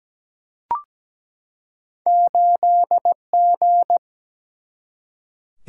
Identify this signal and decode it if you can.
Morse code, heard as a keyed tone, about 17 wpm, 700 Hz. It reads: 8G
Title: Single-Letter Single-Number 17wpm